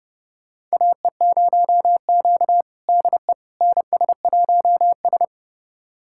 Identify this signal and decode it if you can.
Morse code, heard as a keyed tone, about 30 wpm, 700 Hz. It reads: AE0Q DE NS1S
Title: USA Callsign DE Callsign - 4 Characters - 30wpm